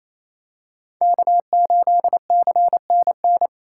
Morse code, heard as a keyed tone, about 28 wpm, 700 Hz. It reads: K8CNN
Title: USA Call Signs - 3x Character Spacing - 28wpm